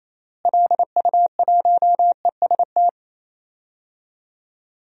Morse code, heard as a keyed tone, about 28 wpm, 700 Hz. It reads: LU1EST
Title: DXCC-100 Call Signs - 6 Characters - Repeated 3x - 28wpm